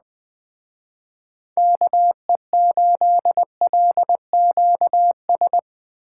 Morse code, heard as a keyed tone, about 21 wpm, 700 Hz. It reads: KE8LQS
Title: Speed Racing - USA 2x3 Call Signs 17wpm